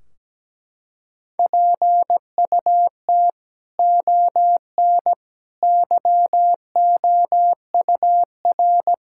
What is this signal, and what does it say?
Morse code, heard as a keyed tone, about 17 wpm, 700 Hz. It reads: PUT ON YOUR
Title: Sentences from Top 1600 Words 17wpm